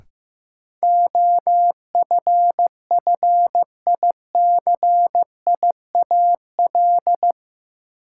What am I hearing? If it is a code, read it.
Morse code, heard as a keyed tone, about 15 wpm, 700 Hz. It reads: OFFICIAL
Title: Sentences from Top 1000 Words - New Word Drill - 15wpm